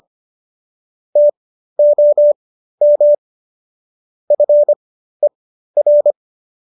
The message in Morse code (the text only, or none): TOM FER